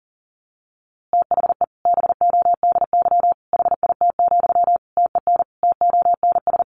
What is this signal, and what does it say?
Morse code, heard as a keyed tone, about 40 wpm, 700 Hz. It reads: THE BODY HIT, TEN TONS
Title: A Sound Of Thunder 40wpm